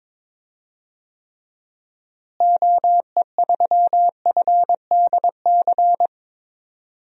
Morse code, heard as a keyed tone, about 22 wpm, 700 Hz. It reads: OE3FDC
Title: DXCC-100 Call Signs - 6 Characters - Repeated 2x - 22wpm